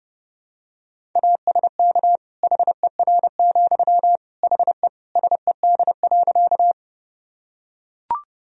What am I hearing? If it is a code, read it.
Morse code, heard as a keyed tone, about 30 wpm, 700 Hz. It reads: ASK HER, HE SED.
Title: Sentences from Top 200 Words and Abbreviations 30wpm